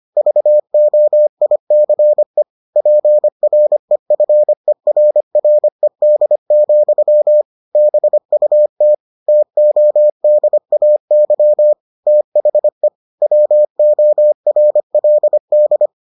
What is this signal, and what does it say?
Morse code, heard as a keyed tone, about 25 wpm, 600 Hz. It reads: VOICE PREFERRED, BUT TODAY THE WORLD